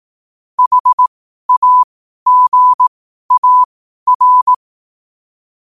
Morse code, heard as a keyed tone, about 18 wpm, spaced as for 12 wpm, 1000 Hz. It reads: HAGAR
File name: qotdfiles / 12wpm